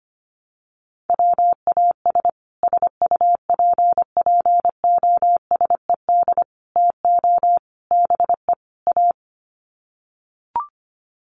Morse code, heard as a keyed tone, about 25 wpm, 700 Hz. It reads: WAS SUPPOSED TO BE A
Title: Sets of 5 Words Encore 25wpm